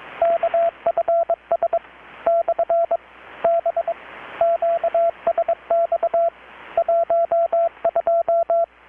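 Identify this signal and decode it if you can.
Morse code, heard as a keyed tone, about 22 wpm, 670 Hz, with radio noise. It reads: KFS / B QSX 12
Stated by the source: Morse from KPH's Night of Nights XX maritime radiotelegraphy event, received in Chiapas, Mexico about 3,800 km away on a Kenwood R-600 with a grounded 10 m long-wire antenna.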